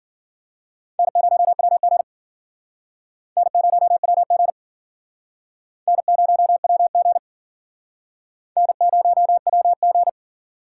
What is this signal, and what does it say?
Morse code, heard as a keyed tone, about 54 wpm, 700 Hz. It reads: N0WG N0WG N0WG N0WG